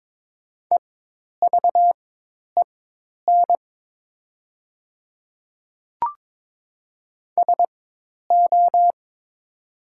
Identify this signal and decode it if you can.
Morse code, heard as a keyed tone, about 22 wpm, spaced as for 10 wpm, 700 Hz. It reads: EVEN SO
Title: 4x Character Spacing - Top 300 Words - 22wpm